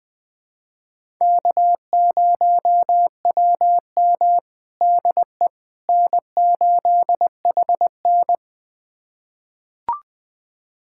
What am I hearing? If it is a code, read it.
Morse code, heard as a keyed tone, about 20 wpm, 700 Hz. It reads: K0WM DE N8HN
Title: USA Callsign DE Callsign - 4 Characters - 20wpm